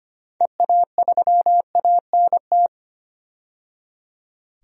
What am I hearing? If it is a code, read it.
Morse code, heard as a keyed tone, about 25 wpm, 700 Hz. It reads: EA3ANT
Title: DXCC-100 Call Signs - 6 Characters - 25wpm